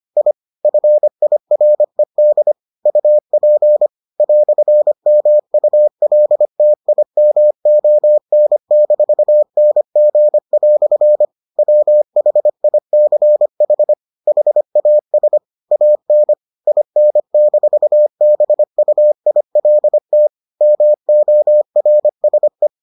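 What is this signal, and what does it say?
Morse code, heard as a keyed tone, about 25 wpm, 600 Hz. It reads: I FIRED UP "MULTIMON-NG" WHICH HAS AN IN-BUILT MORSE
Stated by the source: Morse edition of the Foundations of Amateur Radio episodes.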